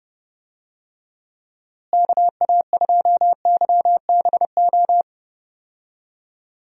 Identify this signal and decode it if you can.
Morse code, heard as a keyed tone, about 30 wpm, 700 Hz. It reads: KA2YBO
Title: USA Call Signs - 6 Characters - Repeated 2x - 30wpm